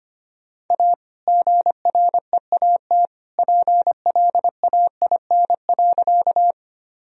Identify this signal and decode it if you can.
Morse code, heard as a keyed tone, about 25 wpm, 700 Hz. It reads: A GREAT PLAIN.
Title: Sentences from Top 1000 Words - Review Part 2 - 25wpm